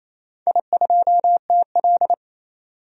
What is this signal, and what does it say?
Morse code, heard as a keyed tone, about 28 wpm, 700 Hz. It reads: I2TL
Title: DXCC-100 Call Signs - 4 Characters - 28wpm